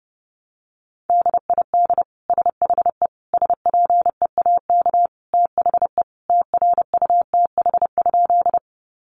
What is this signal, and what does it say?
Morse code, heard as a keyed tone, about 30 wpm, 700 Hz. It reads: DID SHE SPEAK THE TRUTH?
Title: Sentences from Top 1100 Words 30wpm